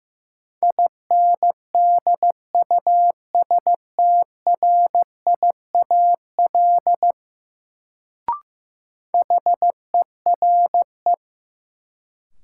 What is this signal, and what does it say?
Morse code, heard as a keyed tone, about 15 wpm, 700 Hz. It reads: INDUSTRIAL HERE
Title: Sets of 5 Words - New Word Drill - 15wpm